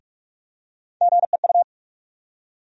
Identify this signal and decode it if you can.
Morse code, heard as a keyed tone, about 42 wpm, 700 Hz. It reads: GEU
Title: USA Call Sign Postfixes - 3 Letters 45wpm